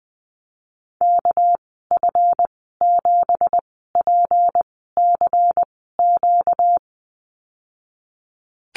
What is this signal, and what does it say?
Morse code, heard as a keyed tone, about 20 wpm, spaced as for 15 wpm, 700 Hz. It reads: KF7PCQ
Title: 238 Lesson 38g - Letter q - Call Signs 20wpm